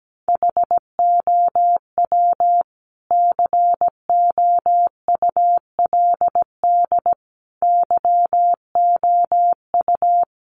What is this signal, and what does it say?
Morse code, heard as a keyed tone, about 17 wpm, 700 Hz. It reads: HOW COULD YOU
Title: Sentences from Top 1100 Words 17wpm